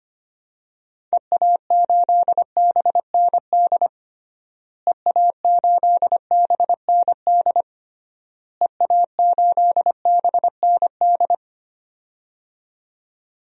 Morse code, heard as a keyed tone, about 25 wpm, 700 Hz. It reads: EA8BND EA8BND EA8BND
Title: DXCC-100 Call Signs - 6 Characters - Repeated 3x - 25wpm